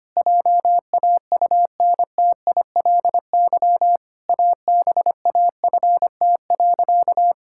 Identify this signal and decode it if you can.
Morse code, heard as a keyed tone, about 25 wpm, 700 Hz. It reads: JAUNTILY ABAFT.
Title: Pangram Challenge v2 - 25wpm